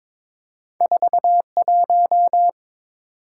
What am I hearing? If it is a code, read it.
Morse code, heard as a keyed tone, about 22 wpm, 700 Hz. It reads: 41
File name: practice / 22wpm / Two Letter-Number Combinations - 22wpm